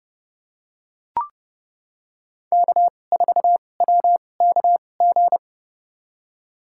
Morse code, heard as a keyed tone, about 30 wpm, spaced as for 22 wpm, 700 Hz. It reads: K4WKG
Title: USA Call Signs - 2x Character Spacing - 30wpm